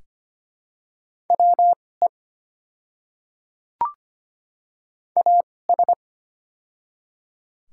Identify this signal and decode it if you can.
Morse code, heard as a keyed tone, about 25 wpm, spaced as for 18 wpm, 700 Hz. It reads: WE AS
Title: 2x Character Spacing - Top 100 Words - 25wpm